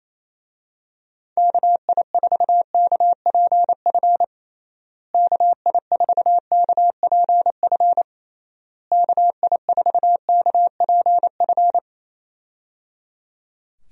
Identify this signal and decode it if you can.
Morse code, heard as a keyed tone, about 27 wpm, 700 Hz. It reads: KI4KPF KI4KPF KI4KPF
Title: USA Call Signs - 6 Characters - Repeated 3x - 28wpm